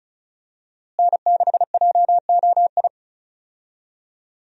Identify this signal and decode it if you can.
Morse code, heard as a keyed tone, about 36 wpm, 700 Hz. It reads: N6JOI
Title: USA Call Signs - 5 Characters - Repeated 2x - 35wpm